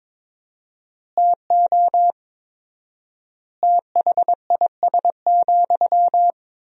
Morse code, heard as a keyed tone, about 22 wpm, 700 Hz. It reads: TO THIS,
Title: 4x Word-Spacing - Sentences from Top 400 Words - 22wpm